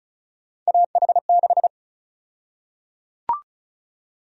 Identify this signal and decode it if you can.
Morse code, heard as a keyed tone, about 34 wpm, 700 Hz. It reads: AH6
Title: USA Call Sign Prefixes - Repeated 2x - 35wpm